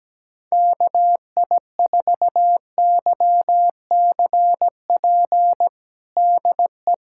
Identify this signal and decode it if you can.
Morse code, heard as a keyed tone, about 17 wpm, 700 Hz. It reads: KI4YCP DE
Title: USA Callsign DE Callsign - 6 Characters - 17wpm